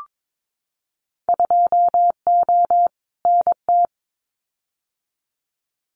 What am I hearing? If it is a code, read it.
Morse code, heard as a keyed tone, about 22 wpm, 700 Hz. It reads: 2O NT
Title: Winter Field Day Exchanges Encore 22wpm